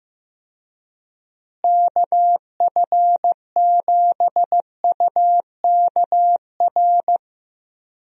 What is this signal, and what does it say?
Morse code, heard as a keyed tone, about 15 wpm, 700 Hz. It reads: KF7UKR
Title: USA Call Signs - 6 Characters - Repeated 2x - 15wpm